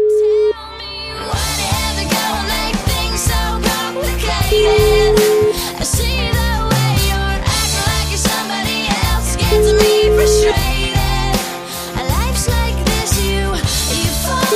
Ringback tone (Europe, 425 Hz), with noise.